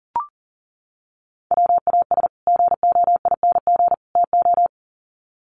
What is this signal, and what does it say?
Morse code, heard as a keyed tone, about 40 wpm, 700 Hz. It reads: WAS GOING TO
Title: Sets of 3 Words 40wpm